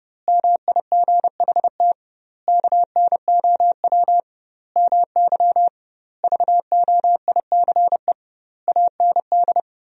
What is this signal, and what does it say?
Morse code, heard as a keyed tone, about 30 wpm, 700 Hz. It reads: MIGHT KNOW MY VOICE AND
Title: 2x Word-Spacing - Sentences from Top 500 Words - 30wpm